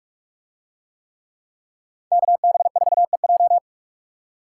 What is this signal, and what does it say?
Morse code, heard as a keyed tone, about 44 wpm, 700 Hz. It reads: KB4EJ